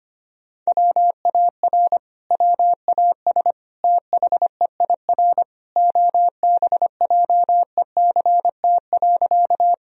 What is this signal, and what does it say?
Morse code, heard as a keyed tone, about 25 wpm, 700 Hz. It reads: WAR WAS THEIR OBJECT.
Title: Sentences from Top 1000 Words - Review Part 3 - 25wpm